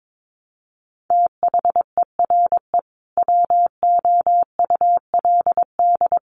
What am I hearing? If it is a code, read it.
Morse code, heard as a keyed tone, about 22 wpm, 700 Hz. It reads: THERE WOULD
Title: Sentences from Top 1200 Words 22wpm